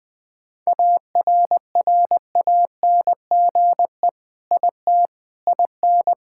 Morse code, heard as a keyed tone, about 21 wpm, 700 Hz. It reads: ARRANGE IT IN
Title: Sentences with 5-Word Phrases - 20wpm